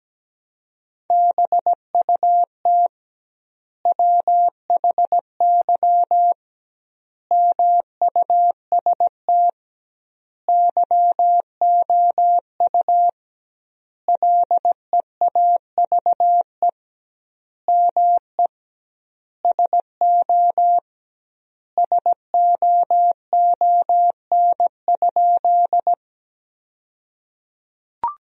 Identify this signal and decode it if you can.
Morse code, heard as a keyed tone, about 17 wpm, 700 Hz. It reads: BUT WHY MUST YOU LEAVE ME SO SOON?
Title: 2x Word-Spacing - Sentences from Top 400 Words - 17wpm